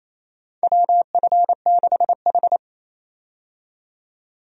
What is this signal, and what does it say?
Morse code, heard as a keyed tone, about 28 wpm, 700 Hz. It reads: WF6H
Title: USA Call Signs - 4 Characters - 28wpm